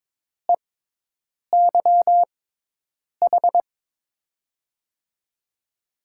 Morse code, heard as a keyed tone, about 22 wpm, spaced as for 8 wpm, 700 Hz. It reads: EYH
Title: USA Call Signs - 5 Characters - 6x Character Spacing - 22wpm